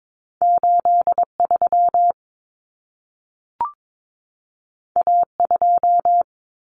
Morse code, heard as a keyed tone, about 22 wpm, 700 Hz. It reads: 83 A2
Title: Common Cut Numbers - 2-Digit 22wpm